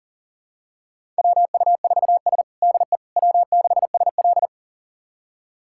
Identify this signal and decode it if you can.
Morse code, heard as a keyed tone, about 40 wpm, 700 Hz. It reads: WU4S DE W6SL
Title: USA Callsign DE Callsign - 4 Characters - 40wpm